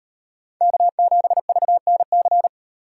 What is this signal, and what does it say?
Morse code, heard as a keyed tone, about 36 wpm, 700 Hz. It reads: K7VNC